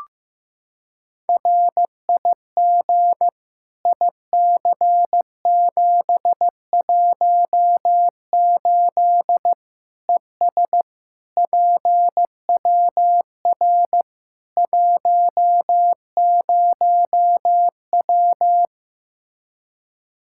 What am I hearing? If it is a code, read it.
Morse code, heard as a keyed tone, about 16 wpm, 700 Hz. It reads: RIG IC718 ES PWR 10W
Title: Practice QSO - Exchange 2 - Info - 15wpm